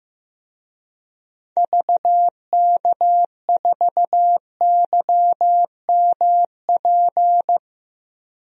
Morse code, heard as a keyed tone, about 16 wpm, 700 Hz. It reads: VK4YMP
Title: DXCC-100 Call Signs - 6 Characters - 15wpm